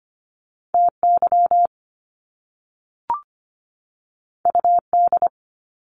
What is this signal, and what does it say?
Morse code, heard as a keyed tone, about 25 wpm, 700 Hz. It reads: TY UD
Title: Top Two Letter Combinations in Words 25wpm